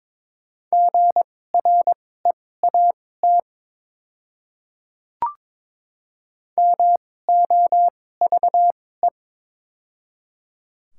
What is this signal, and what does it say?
Morse code, heard as a keyed tone, about 21 wpm, spaced as for 16 wpm, 700 Hz. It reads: GREAT MOVE
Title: 2x Character Spacing - Top 200 Words - 22wpm